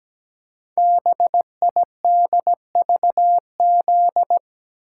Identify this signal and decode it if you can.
Morse code, heard as a keyed tone, about 17 wpm, 700 Hz. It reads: BIDVZ
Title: Code Groups - 5 Letter:Numbers - 17wpm